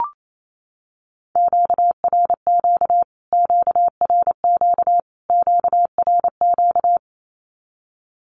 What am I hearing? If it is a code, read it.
Morse code, heard as a keyed tone, about 28 wpm, 700 Hz. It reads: QRQ QRQ QRQ
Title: Q-Codes Repeated 3x 28wpm